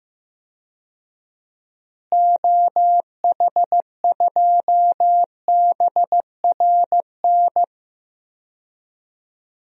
Morse code, heard as a keyed tone, about 15 wpm, 700 Hz. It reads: OH2BRN
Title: DXCC-100 Call Signs - 6 Characters - Repeated 3x - 15wpm